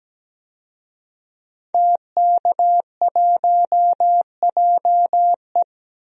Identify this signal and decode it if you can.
Morse code, heard as a keyed tone, about 17 wpm, 700 Hz. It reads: TK1JE
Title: DXCC-100 Call Signs - 5 Characters - 17wpm